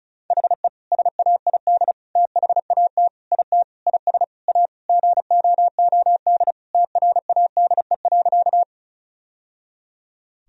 Morse code, heard as a keyed tone, about 36 wpm, 700 Hz. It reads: HE SAID THAT IT IS A GOOD TRADE.
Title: Sentences from Top 1000 Words - Review Part 4 - 35wpm